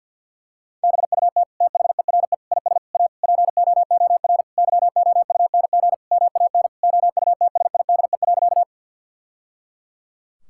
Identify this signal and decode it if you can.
Morse code, heard as a keyed tone, about 51 wpm, 700 Hz. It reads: BUT THERE IS A POOR YOUNG MAN OUTSIDE.
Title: Sentences from Top 1000 Words - Review Part 6 - 50wpm